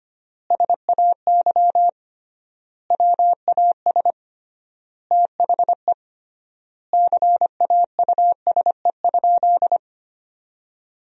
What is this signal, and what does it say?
Morse code, heard as a keyed tone, about 25 wpm, 700 Hz. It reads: SAY WAS THE CAUSE?
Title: 3x Word-Spacing - Sentences from Top 200 Words - 25wpm